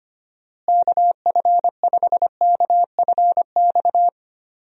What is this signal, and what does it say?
Morse code, heard as a keyed tone, about 25 wpm, 700 Hz. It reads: KF5KFX
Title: USA Call Signs - 6 Characters - Repeated 3x - 25wpm